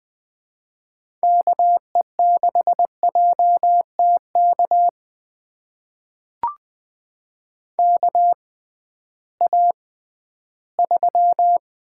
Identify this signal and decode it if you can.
Morse code, heard as a keyed tone, about 20 wpm, spaced as for 13 wpm, 700 Hz. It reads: KE6JTK KA3
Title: USA Call Signs - 6 Characters - 6x Character Spacing - 20wpm